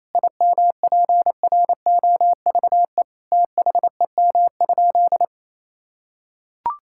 Morse code, heard as a keyed tone, about 27 wpm, 700 Hz. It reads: IMPROVE THEM?
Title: Sentences from Top 1000 Words - Review Part 5 - 28wpm